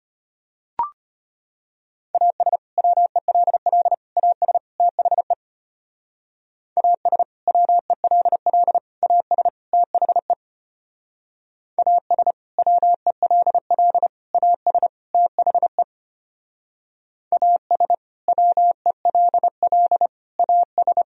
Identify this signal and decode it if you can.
Morse code, heard as a keyed tone, about 32 wpm, 700 Hz. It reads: AS WELL AS THE AS WELL AS THE AS WELL AS THE AS WELL AS